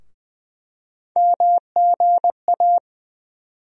Morse code, heard as a keyed tone, about 20 wpm, 700 Hz. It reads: MGA
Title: USA Call Sign Postfixes - 3 Letters 20wpm